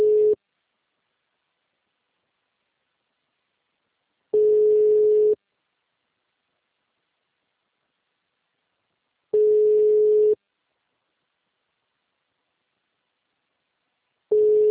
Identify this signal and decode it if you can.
Ringback tone (Europe, 425 Hz).